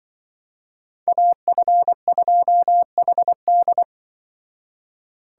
Morse code, heard as a keyed tone, about 23 wpm, 700 Hz. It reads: AF2HD
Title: Speed Racing - USA 2x2 Call Signs 20wpm